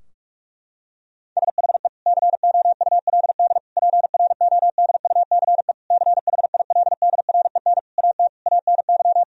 Morse code, heard as a keyed tone, about 41 wpm, 700 Hz. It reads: SHE COULD PRODUCE CHILDREN AT ANY
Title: Sentences from Top 1000 Words - Review Part 3 - 45wpm